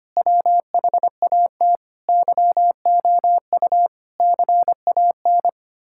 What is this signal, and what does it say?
Morse code, heard as a keyed tone, about 25 wpm, 700 Hz. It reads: WHAT YOU CAN
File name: practice / 25wpm / Sentences with 9-Word Phrases - 25wpm